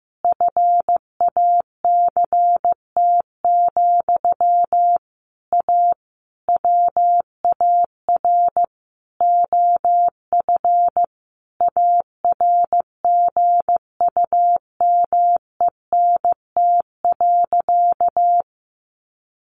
Morse code, heard as a keyed tone, about 15 wpm, 700 Hz. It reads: FACT, A WAR OF ARGUMENT.
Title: Sentences from Top 1400 Words 15wpm